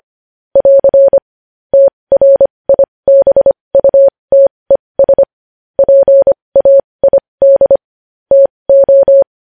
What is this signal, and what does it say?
Morse code, heard as a keyed tone, about 25 wpm, spaced as for 20 wpm, 550 Hz. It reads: <AR> TRIBUTES PAID TO
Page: news-headlines-in-morse-code-at-20-wpm